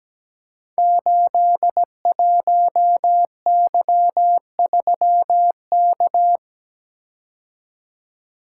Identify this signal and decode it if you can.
Morse code, heard as a keyed tone, about 17 wpm, 700 Hz. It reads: 81Y3K
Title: Code Groups - 5 Letter:Numbers - 17wpm